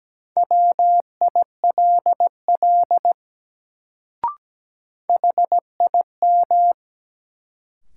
Morse code, heard as a keyed tone, about 17 wpm, 700 Hz. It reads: WILL HIM
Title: Sentences from Top 100 Words - New Word Drill - 17wpm